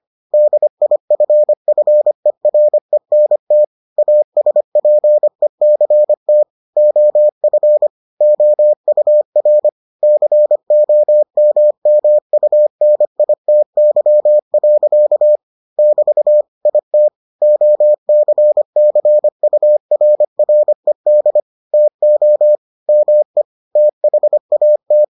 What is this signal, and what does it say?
Morse code, heard as a keyed tone, about 25 wpm, 600 Hz. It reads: DIFFERENT ASPECT OF OUR COMMUNITY. <BT> IT OCCURRED TO ME THAT